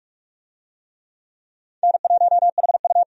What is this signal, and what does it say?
Morse code, heard as a keyed tone, about 42 wpm, 700 Hz. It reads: N1HU
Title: USA Call Signs - 4 Characters - 45wpm